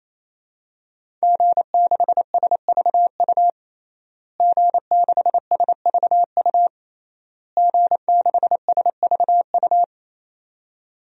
Morse code, heard as a keyed tone, about 28 wpm, 700 Hz. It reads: G6SVU G6SVU G6SVU
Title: DXCC-100 Call Signs - 5 Characters - Repeated 3x - 28wpm